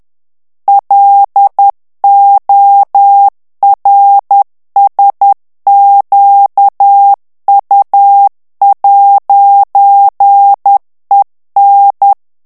Morse code, heard as a keyed tone, about 10 wpm, 800 Hz. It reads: LORSQU'EN
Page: Français - texte en clair 1 - vitesse : 10 mots minute